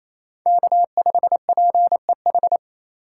Morse code, heard as a keyed tone, about 28 wpm, 700 Hz. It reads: K5PEH
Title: USA Call Signs - 5 Characters - 2x Character Spacing - 28wpm